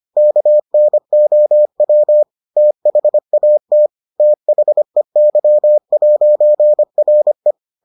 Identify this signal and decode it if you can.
Morse code, heard as a keyed tone, about 25 wpm, 600 Hz. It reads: KNOW THAT THEY'RE